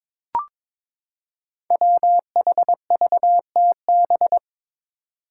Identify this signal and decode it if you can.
Morse code, heard as a keyed tone, about 22 wpm, 700 Hz. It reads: WHVTB
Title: Five Challenging Letter-Number Combinations 22wpm